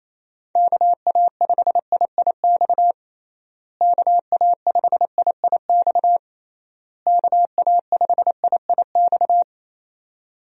Morse code, heard as a keyed tone, about 28 wpm, 700 Hz. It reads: KA5IIX KA5IIX KA5IIX
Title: USA Call Signs - 6 Characters - Repeated 3x - 28wpm